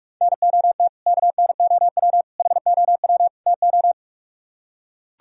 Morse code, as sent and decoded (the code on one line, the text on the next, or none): -. --- - / -.- -. --- .-- / .... --- .-- / - ---
NOT KNOW HOW TO